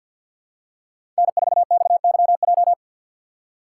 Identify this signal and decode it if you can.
Morse code, heard as a keyed tone, about 47 wpm, 700 Hz. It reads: N4XYJ
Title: USA Call Signs - 5 Characters - 4x Character Spacing - 50wpm